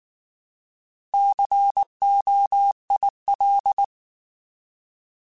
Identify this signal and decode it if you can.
Morse code, heard as a keyed tone, about 19 wpm, 800 Hz. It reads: COIL
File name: Alphabetical / Alphabetical mp3s / Cs @18